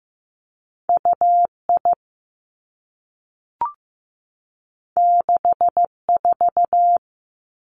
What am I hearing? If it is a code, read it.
Morse code, heard as a keyed tone, about 15 wpm, 700 Hz. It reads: U I 6 4
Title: Two Challenging Letter-Number Combinations 15wpm